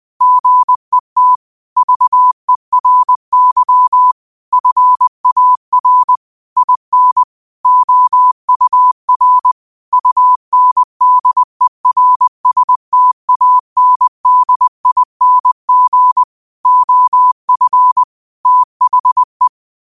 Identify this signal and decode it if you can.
Morse code, heard as a keyed tone, about 20 wpm, 1000 Hz. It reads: GET VERY FAR IN OUR UNDERSTANDING OF THE
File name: qotdfiles / 20wpm